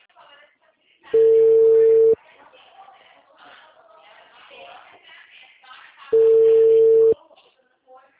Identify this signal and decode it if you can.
Ringback tone (China, 450 Hz), with speech.